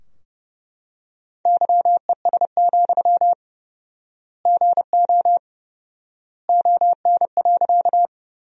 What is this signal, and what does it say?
Morse code, heard as a keyed tone, about 30 wpm, 700 Hz. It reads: YES, GO ON.
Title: 4x Word-Spacing - Sentences from Top 500 Words - 30wpm